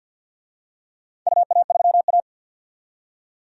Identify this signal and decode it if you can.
Morse code, heard as a keyed tone, about 51 wpm, 700 Hz. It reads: UA3A
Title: DXCC-100 Call Signs - 4 Characters - 50wpm